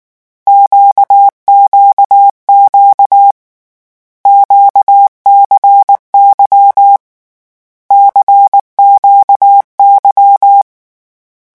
Morse code, heard as a keyed tone, about 19 wpm, 800 Hz. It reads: QQQ QCY CQY
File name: morse mp3 / lecon2-vitesse 18